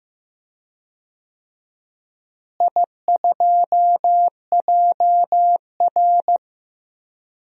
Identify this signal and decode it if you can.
Morse code, heard as a keyed tone, about 15 wpm, 700 Hz. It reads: I2JR
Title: DXCC-100 Call Signs - 4 Characters - Repeated 2x - 15wpm